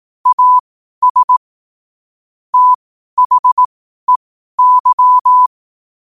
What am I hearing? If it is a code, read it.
Morse code, heard as a keyed tone, about 18 wpm, spaced as for 12 wpm, 1000 Hz. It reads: AS THEY
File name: qotdfiles / 12wpm